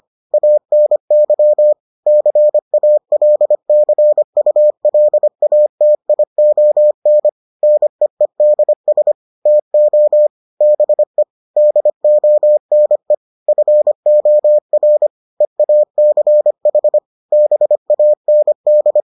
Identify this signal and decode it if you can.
Morse code, heard as a keyed tone, about 25 wpm, 600 Hz. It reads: ANY CALCULATION NEEDS TO BE DONE FOR EACH BAND